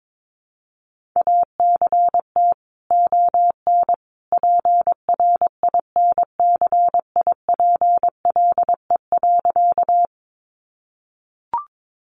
Morse code, heard as a keyed tone, about 22 wpm, 700 Hz. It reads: ACT ON PRINCIPLE.
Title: Sentences from Top 1400 Words 22wpm